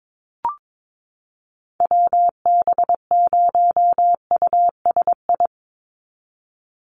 Morse code, heard as a keyed tone, about 22 wpm, 700 Hz. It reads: WB0USI
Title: NA SOTA Activator Callsigns 22wpm